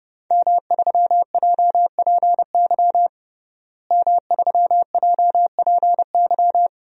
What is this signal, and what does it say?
Morse code, heard as a keyed tone, about 30 wpm, 700 Hz. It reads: M3JPY M3JPY
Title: DXCC-100 Call Signs - 5 Characters - Repeated 3x - 30wpm